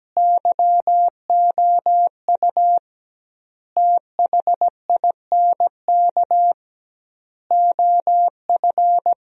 Morse code, heard as a keyed tone, about 17 wpm, 700 Hz. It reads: YOU THINK OF
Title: 2x Word-Spacing - Sentences from Top 200 Words - 17wpm